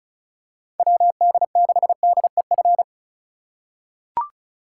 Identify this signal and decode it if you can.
Morse code, heard as a keyed tone, about 34 wpm, 700 Hz. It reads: WD6DEF
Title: USA Call Signs - 2x Character Spacing - 35wpm